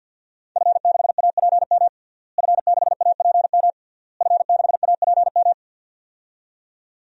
Morse code, heard as a keyed tone, about 50 wpm, 700 Hz. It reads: F6APM F6APM F6APM
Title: DXCC-100 Call Signs - 5 Characters - Repeated 3x - 50wpm